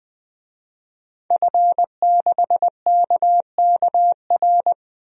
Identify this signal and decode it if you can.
Morse code, heard as a keyed tone, about 20 wpm, 700 Hz. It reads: F6KKR
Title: DXCC-100 Call Signs - 5 Characters - 20wpm